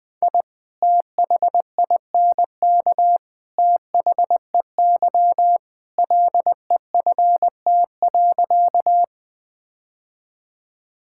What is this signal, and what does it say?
Morse code, heard as a keyed tone, about 21 wpm, 700 Hz. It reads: I THINK THEY LEFT.
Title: Sentences with 3-Word Phrases - Spoken Beginning - 20wpm